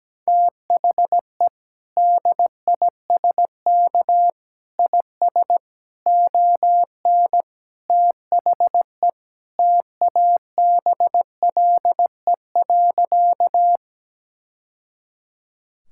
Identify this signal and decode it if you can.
Morse code, heard as a keyed tone, about 17 wpm, 700 Hz. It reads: THE DISK IS ON THE TABLE.
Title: Sentences from Top 1000 Words - Review Part 6 - 17wpm